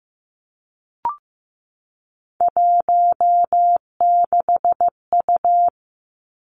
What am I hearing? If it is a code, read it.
Morse code, heard as a keyed tone, about 15 wpm, 700 Hz. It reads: 16U
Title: Three Challenging Letter-Number Combinations 15wpm